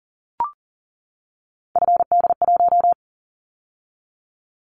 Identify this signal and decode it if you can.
Morse code, heard as a keyed tone, about 40 wpm, 700 Hz. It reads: FD1
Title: Callsign Prefixes for Top 25 DXCC - 40wpm